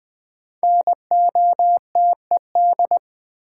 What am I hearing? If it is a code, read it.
Morse code, heard as a keyed tone, about 20 wpm, 700 Hz. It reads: NOTED
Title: Sets of 5 Words - New Word Drill - 20wpm